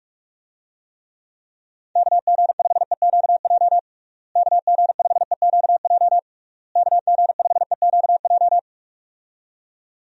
Morse code, heard as a keyed tone, about 46 wpm, 700 Hz. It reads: KG5EQJ KG5EQJ KG5EQJ
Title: USA Call Signs - 6 Characters - Repeated 3x - 45wpm